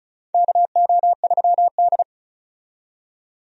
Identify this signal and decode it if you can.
Morse code, heard as a keyed tone, about 34 wpm, 700 Hz. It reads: KO3D
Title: USA Call Signs - 4 Characters - 35wpm